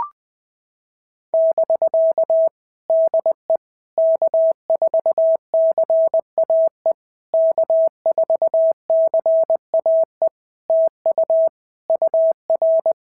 Morse code, heard as a keyed tone, about 20 wpm, 650 Hz. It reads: <BK> DE K4CAE K4CAE TU UR
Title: POTA QSO Practice - 20wpm